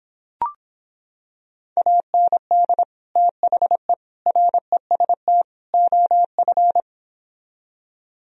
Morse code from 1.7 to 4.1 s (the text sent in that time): AND THE